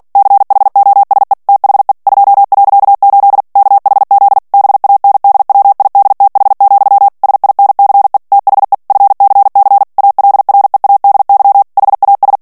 Morse code, heard as a keyed tone, about 48 wpm, 800 Hz. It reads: CHOSE THE 2.8 KHZ BANDWIDTH, SINCE THE FCC ALREADY HAS
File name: ANGLAIS texte clair 3 vitesse 45 mots minute